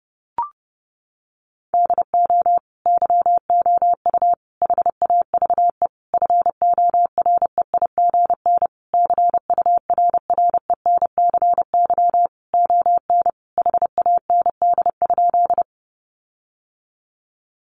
Morse code, heard as a keyed tone, about 30 wpm, 700 Hz. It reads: DO YOU HAVE FOREIGN CURRENCY ON HAND?
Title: Sentences from Top 1500 Words 30wpm